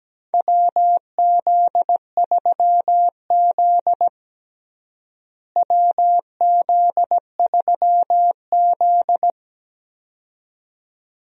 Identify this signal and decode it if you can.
Morse code, heard as a keyed tone, about 17 wpm, 700 Hz. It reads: WZ3Z WZ3Z
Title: USA Call Signs - 4 Characters - Repeated 2x - 17wpm